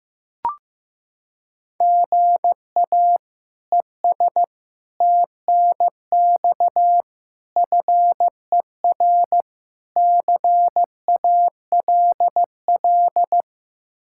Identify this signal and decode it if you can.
Morse code, heard as a keyed tone, about 16 wpm, 700 Hz. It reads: GA ES TNX FER CALL
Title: K1USN SST QSO Practice 15wpm